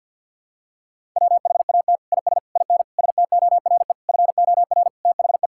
Morse code, heard as a keyed tone, about 49 wpm, 700 Hz. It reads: WHAT IS IN STORE FOR THE
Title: Sentences from Top 1000 Words - Review Part 5 - 50wpm